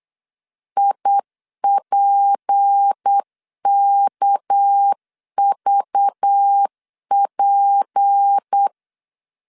Morse code, heard as a keyed tone, about 8 wpm, 800 Hz. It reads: IPKVP
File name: CW / DET / 1 6 1500C S C 8wpm 5219